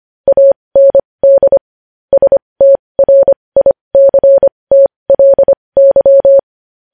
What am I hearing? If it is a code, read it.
Morse code, heard as a keyed tone, about 25 wpm, spaced as for 20 wpm, 550 Hz. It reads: AND STRICTLY
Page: News headlines in Morse code at 20 WPM